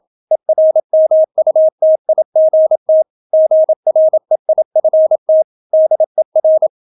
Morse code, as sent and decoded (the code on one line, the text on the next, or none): . .-. -- ..- - .. --. - / --. .-. .. ..-. - / -.. .-.
ERMUTIGT GREIFT DER